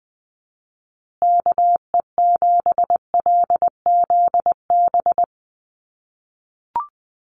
Morse code, heard as a keyed tone, about 20 wpm, 700 Hz. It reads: KE7LZB
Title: USA Call Signs Repeated 2x 20wpm